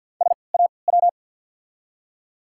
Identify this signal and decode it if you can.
Morse code, heard as a keyed tone, about 53 wpm, spaced as for 29 wpm, 700 Hz. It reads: SAW
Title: 3x Character Spacing - Top 500 Words - 50wpm